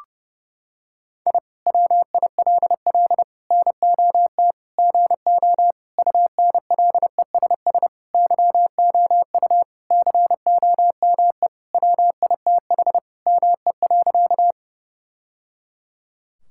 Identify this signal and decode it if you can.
Morse code, heard as a keyed tone, about 30 wpm, 700 Hz. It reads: I WILL NOT GO UNLESS YOU COME WITH ME.
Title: Sentences from Top 1000 Words - Review Part 6 - 30wpm